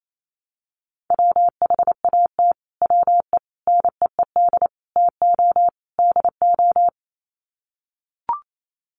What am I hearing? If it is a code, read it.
Morse code, heard as a keyed tone, about 28 wpm, 700 Hz. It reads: WHAT WE NEED TO DO
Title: Sets of 5 Words Encore 28wpm